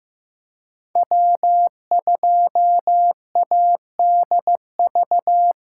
Morse code, heard as a keyed tone, about 15 wpm, 700 Hz. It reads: W2ADV
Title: USA Callsign DE Callsign - 5 Characters - 15wpm